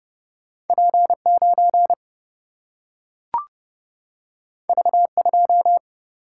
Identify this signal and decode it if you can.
Morse code, heard as a keyed tone, about 30 wpm, 700 Hz. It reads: P9 V2
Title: Single-Letter Single-Number 30wpm